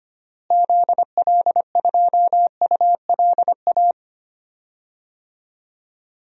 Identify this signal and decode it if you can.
Morse code, heard as a keyed tone, about 25 wpm, 700 Hz. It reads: ZL2ULA
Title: DXCC-100 Call Signs - 6 Characters - Repeated 2x - 25wpm